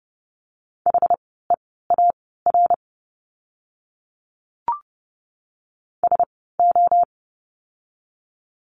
Morse code, heard as a keyed tone, about 30 wpm, spaced as for 17 wpm, 700 Hz. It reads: HEAR SO
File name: course / 30:10wpm / 237 Lesson 38f - Letter q - Words 30wpm